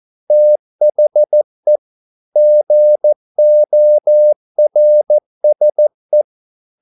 Morse code, heard as a keyed tone, about 14 wpm, 600 Hz. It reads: THE GORSE
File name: mp3 files / a dogs mistake - 14wpm